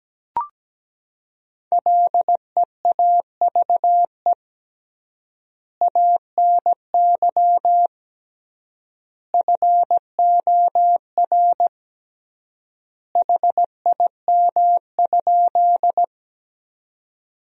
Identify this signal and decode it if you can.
Morse code, heard as a keyed tone, about 17 wpm, 700 Hz. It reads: LEAVE ANY FOR HIM?
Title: 3x Word-Spacing - Sentences from Top 400 Words - 17wpm